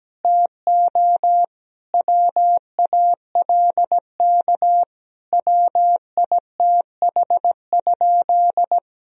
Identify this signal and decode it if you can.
Morse code, heard as a keyed tone, about 17 wpm, 700 Hz. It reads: TO WALK WITH?
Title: Sentences from Top 1000 Words - Review Part 1 - 17wpm